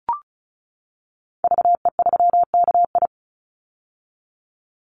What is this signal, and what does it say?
Morse code, heard as a keyed tone, about 34 wpm, 700 Hz. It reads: VE3KI
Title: Call Signs - 35wpm